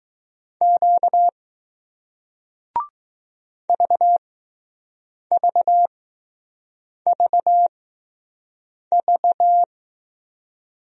Morse code, heard as a keyed tone, about 20 wpm, spaced as for 6 wpm, 700 Hz. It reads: Q VVVV